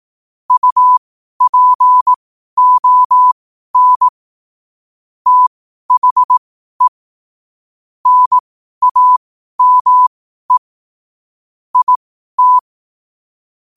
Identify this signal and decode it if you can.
Morse code, heard as a keyed tone, about 18 wpm, spaced as for 12 wpm, 1000 Hz. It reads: UPON THE NAME IT
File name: qotdfiles / 12wpm